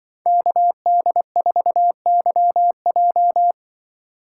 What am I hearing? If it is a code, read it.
Morse code, heard as a keyed tone, about 25 wpm, 700 Hz. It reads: KD4YJ
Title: Speed Racing - USA 2x2 Call Signs 20wpm